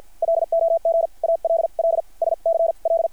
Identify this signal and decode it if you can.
Morse code, heard as a keyed tone, about 50 wpm, spaced as for 42 wpm, 660 Hz. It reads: LOG ALL SCL
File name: archived / CRSnet